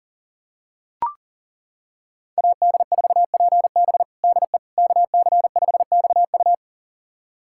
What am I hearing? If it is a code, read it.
Morse code, heard as a keyed tone, about 37 wpm, 690 Hz. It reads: AD4PB DE KC5XU
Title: USA Callsign DE Callsign - 5 Characters - 40wpm